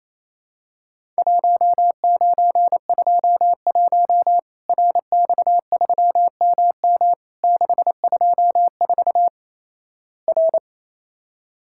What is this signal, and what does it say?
Morse code, heard as a keyed tone, about 28 wpm, 700 Hz. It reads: 1921 RX3MM 624 R
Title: Worked All Europe DX Contest - QTC Exchange - 28wpm